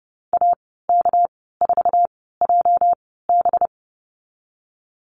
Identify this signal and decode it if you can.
Morse code, heard as a keyed tone, about 30 wpm, spaced as for 17 wpm, 700 Hz. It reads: AK4JB
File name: Lesson 37g - Letter x - Call Signs 30wpm